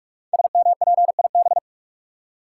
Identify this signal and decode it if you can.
Morse code, heard as a keyed tone, about 44 wpm, 700 Hz. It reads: SMPIB
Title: Code Groups - 5 Letters - 45wpm